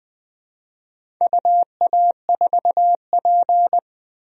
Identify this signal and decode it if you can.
Morse code, heard as a keyed tone, about 20 wpm, 700 Hz. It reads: UA4P
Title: DXCC-100 Call Signs - 4 Characters - 20wpm